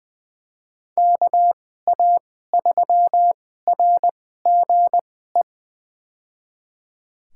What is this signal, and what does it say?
Morse code, heard as a keyed tone, about 20 wpm, spaced as for 14 wpm, 700 Hz. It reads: KA3RGE
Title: USA Call Signs - 6 Characters - 2x Character Spacing - 20wpm